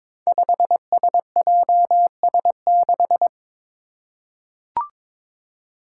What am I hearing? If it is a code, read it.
Morse code, heard as a keyed tone, about 22 wpm, 700 Hz. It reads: 5SJS6
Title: Five Challenging Letter-Number Combinations 22wpm